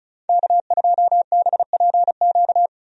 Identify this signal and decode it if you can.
Morse code, heard as a keyed tone, about 36 wpm, 700 Hz. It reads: K2BPQ
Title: USA Call Signs - 5 Characters - 35wpm